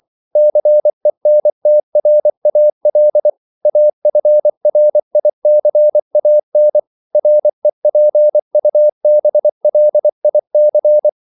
Morse code, heard as a keyed tone, about 24 wpm, 600 Hz. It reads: CENTRAL AFRICAN REPUBLIC
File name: CW / Practice / NationalCapitals24